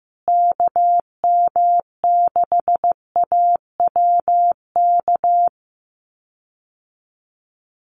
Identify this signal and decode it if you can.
Morse code, heard as a keyed tone, about 15 wpm, 700 Hz. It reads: KM6AWK
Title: USA Call Signs Repeated 2x 15wpm